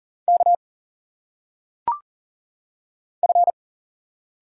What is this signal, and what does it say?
Morse code, heard as a keyed tone, about 40 wpm, 700 Hz. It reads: K F
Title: Single Letters - Phonetic Pronunciation 40wpm